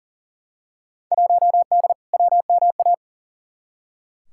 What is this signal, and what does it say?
Morse code, heard as a keyed tone, about 40 wpm, 700 Hz. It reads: 1D WMA
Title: Field Day Exchanges - 40wpm